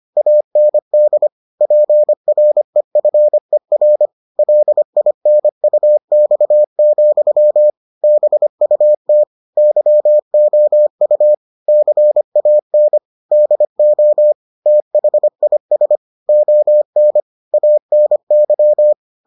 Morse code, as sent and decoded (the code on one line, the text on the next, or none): .- -. -.. / .--. .-. ..-. .-. / .-.. .. -. ..- -..- --..-- / -... ..- - / -.-- --- ..- / -.-. .- -. / -.. --- / - .... .. ... / --- -. / .- -. -.--
AND PREFER LINUX, BUT YOU CAN DO THIS ON ANY